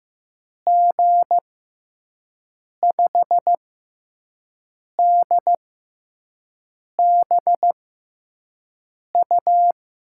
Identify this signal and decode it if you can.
Morse code, heard as a keyed tone, about 15 wpm, spaced as for 5 wpm, 700 Hz. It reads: G5DBU
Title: USA Call Signs - 6 Characters - 6x Character Spacing - 15wpm